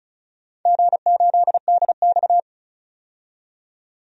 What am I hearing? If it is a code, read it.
Morse code, heard as a keyed tone, about 36 wpm, 700 Hz. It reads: G8DX
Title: DXCC-100 Call Signs - 4 Characters - 35wpm